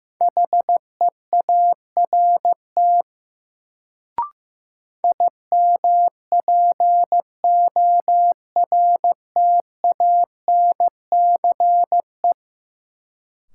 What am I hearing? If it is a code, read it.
Morse code, heard as a keyed tone, about 15 wpm, 700 Hz. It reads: HEART IMPORTANCE
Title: Sentences from Top 1200 Words - New Word Drill - 15wpm